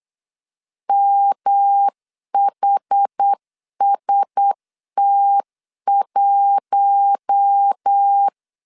Morse code, heard as a keyed tone, about 9 wpm, 800 Hz. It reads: MHST1
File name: CW / DET / 1 6 1500C S C 8wpm 5219